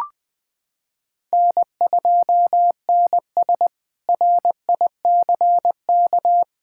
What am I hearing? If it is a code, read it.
Morse code, heard as a keyed tone, about 20 wpm, 700 Hz. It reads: N2NS RICK
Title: K1USN SST QSO Practice - 20wpm